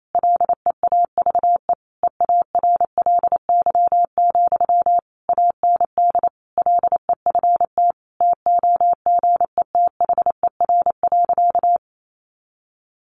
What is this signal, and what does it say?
Morse code, heard as a keyed tone, about 28 wpm, 700 Hz. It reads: LEAVE EARLY, AND LEFT TOGETHER.
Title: Sentences from Top 400 Words 28wpm